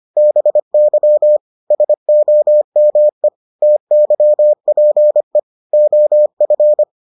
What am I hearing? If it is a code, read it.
Morse code, heard as a keyed tone, about 25 wpm, 600 Hz. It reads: BY SOME TYPE OF